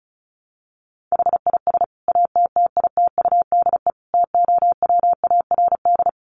Morse code, heard as a keyed tone, about 34 wpm, 700 Hz. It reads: HIS ATTITUDE TOWARD
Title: Sentences from Top 1300 Words 35wpm